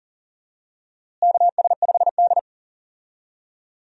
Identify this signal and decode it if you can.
Morse code, heard as a keyed tone, about 40 wpm, 700 Hz. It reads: KS5D
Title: USA Call Signs - 4 Characters - Repeated 3x - 40wpm